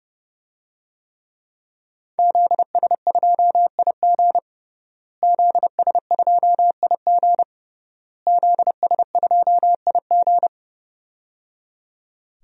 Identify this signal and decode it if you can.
Morse code, heard as a keyed tone, about 28 wpm, 700 Hz. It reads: ZS2IG ZS2IG ZS2IG
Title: DXCC-100 Call Signs - 5 Characters - Repeated 3x - 30wpm